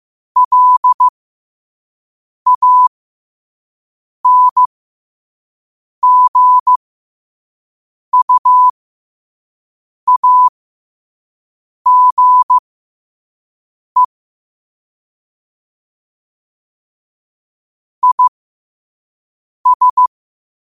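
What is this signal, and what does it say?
Morse code, heard as a keyed tone, about 15 wpm, spaced as for 5 wpm, 1000 Hz. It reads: LANGUAGE IS
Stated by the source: Morse code sent at 5 wpm.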